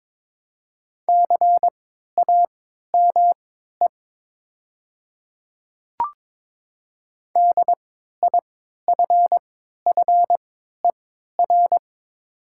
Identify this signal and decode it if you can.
Morse code, heard as a keyed tone, about 22 wpm, spaced as for 12 wpm, 700 Hz. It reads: CAME DIFFER
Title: 3x Character Spacing - Top 200 Words - 22wpm